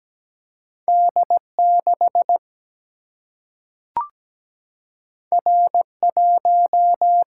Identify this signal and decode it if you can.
Morse code, heard as a keyed tone, about 17 wpm, 700 Hz. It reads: D6 R1
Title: Single-Letter Single-Number 17wpm